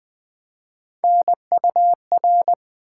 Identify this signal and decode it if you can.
Morse code, heard as a keyed tone, about 20 wpm, 700 Hz. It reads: NUR
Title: Build A Word - 1 Letter At A Time - 5-Letter Words - 20wpm